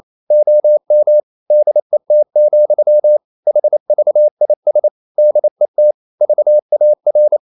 Morse code, heard as a keyed tone, about 28 wpm, 600 Hz. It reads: OM DET, HVIS DET VAR
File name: Ferie QRQ 28wpm